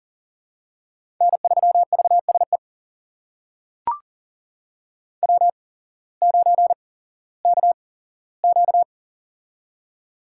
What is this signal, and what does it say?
Morse code, heard as a keyed tone, about 41 wpm, spaced as for 19 wpm, 700 Hz. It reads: N3VSE W9KQ
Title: USA Call Signs - 8x Character Spacing - 40wpm